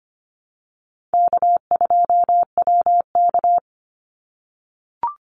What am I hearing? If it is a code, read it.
Morse code, heard as a keyed tone, about 25 wpm, 700 Hz. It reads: K2WK
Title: Call Signs - 25wpm